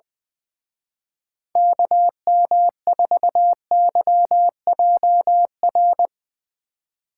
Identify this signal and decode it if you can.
Morse code, heard as a keyed tone, about 21 wpm, 700 Hz. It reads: KM4YJR
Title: Speed Racing - USA 2x3 Call Signs 20wpm